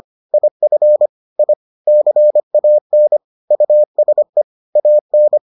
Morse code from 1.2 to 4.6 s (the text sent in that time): I CAN USE